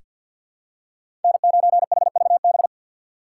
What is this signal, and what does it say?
Morse code, heard as a keyed tone, about 47 wpm, 700 Hz. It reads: N9HVB
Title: USA Call Signs - 6x Character Spacing - 50wpm